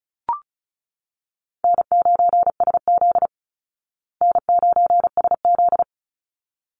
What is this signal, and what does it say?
Morse code, heard as a keyed tone, about 34 wpm, 700 Hz. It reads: N9SZ N9SZ
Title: Easy Call Signs Repeated 2x - 35wpm